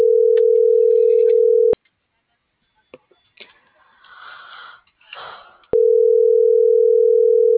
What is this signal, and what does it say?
Ringback tone (North America), 440 + 480 Hz.